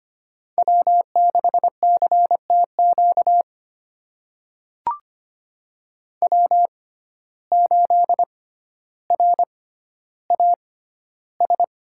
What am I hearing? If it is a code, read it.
Morse code, heard as a keyed tone, about 25 wpm, spaced as for 13 wpm, 700 Hz. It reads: W6CTQ W8RAS
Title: USA Call Signs - 5 Characters - 6x Character Spacing - 25wpm